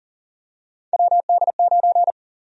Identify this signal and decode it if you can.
Morse code, heard as a keyed tone, about 40 wpm, 700 Hz. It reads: WD9
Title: USA Call Sign Prefixes - Repeated 2x - 40wpm